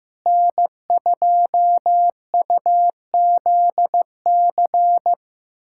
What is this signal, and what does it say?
Morse code, heard as a keyed tone, about 15 wpm, 700 Hz. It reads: N2UZC
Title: USA Callsign DE Callsign - 5 Characters - 15wpm